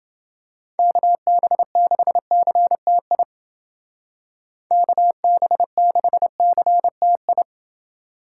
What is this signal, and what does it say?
Morse code, heard as a keyed tone, about 29 wpm, 700 Hz. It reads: KB6CTI KB6CTI